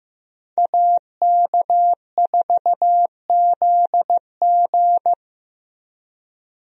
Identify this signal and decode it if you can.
Morse code, heard as a keyed tone, about 15 wpm, 700 Hz. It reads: AK4ZG
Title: USA Call Signs - 5 Characters - Repeated 2x - 15wpm